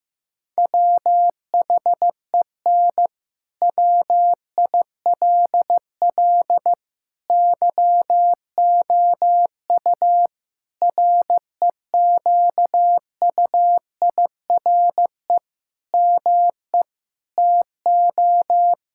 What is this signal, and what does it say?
Morse code, heard as a keyed tone, about 16 wpm, 700 Hz. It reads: WHEN WILL YOU REQUIRE ME TO
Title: Sentences from Top 1000 Words - Review Part 6 - 15wpm